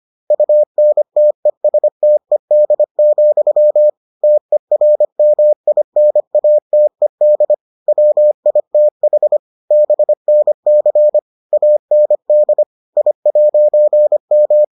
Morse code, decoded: UNTESTED, TERMINATED WITH BNC AND I'M

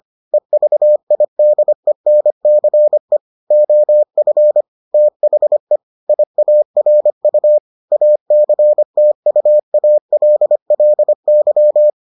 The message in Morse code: EVIDENCE OF THE IARU ACTUALLY

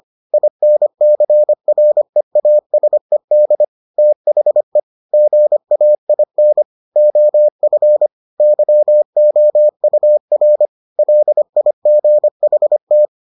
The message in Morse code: INCREASED THE GAIN OF YOUR LIGHT

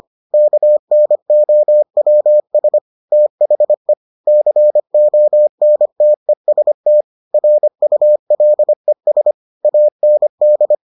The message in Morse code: KNOWS THE CONTEST RULES AND

